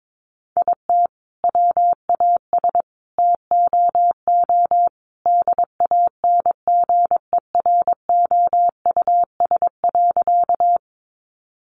Morse code, heard as a keyed tone, about 22 wpm, 700 Hz. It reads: IT WAS TOO DANGEROUS.
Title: Sentences from Top 1200 Words 22wpm